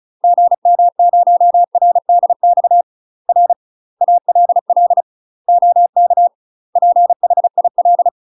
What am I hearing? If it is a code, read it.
Morse code, heard as a keyed tone, about 35 wpm, 700 Hz. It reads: GM0RDX R ALL OK PHIL
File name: UK MASTER QSOs TRAINING / 35WPM / UK MASTER QSOs 0032